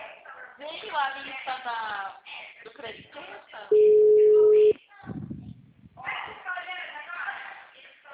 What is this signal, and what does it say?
Ringback tone (Europe, 425 Hz), with speech.